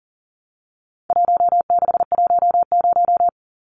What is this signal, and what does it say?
Morse code, heard as a keyed tone, about 38 wpm, 700 Hz. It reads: 1610
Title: Numbers - 4-Digit 40wpm